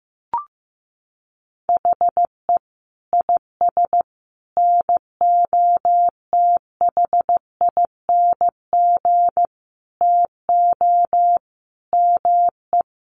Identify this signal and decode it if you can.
Morse code, heard as a keyed tone, about 15 wpm, 700 Hz. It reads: HE IS NOTHING TO ME
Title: Sentences from Top 1200 Words 15wpm